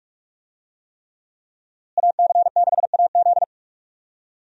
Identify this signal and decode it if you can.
Morse code, heard as a keyed tone, about 46 wpm, 700 Hz. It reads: AC6AZ
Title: USA Call Signs - 5 Characters - Repeated 3x - 45wpm